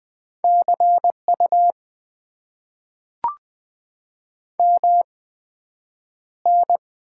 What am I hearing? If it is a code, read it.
Morse code, heard as a keyed tone, about 20 wpm, spaced as for 9 wpm, 700 Hz. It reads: CU MN
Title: Top 90 QSO Elements - 8x Character Spacing - 20wpm